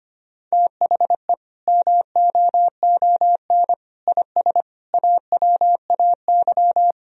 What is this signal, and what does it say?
Morse code, heard as a keyed tone, about 25 wpm, 700 Hz. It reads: THE MOON IS AWAY
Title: Sentences from Top 1000 Words - Review Part 3 - 25wpm